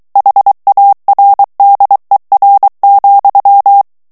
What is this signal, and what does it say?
Morse code, heard as a keyed tone, about 23 wpm, 800 Hz. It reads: HARDER,
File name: morse mp3 / ANGLAIS texte clair 1 vitesse 22 mots minute